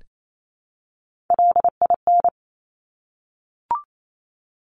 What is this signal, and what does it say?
Morse code, heard as a keyed tone, about 28 wpm, 700 Hz. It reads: LIN
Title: Top Three Letter Combinations in Words 28wpm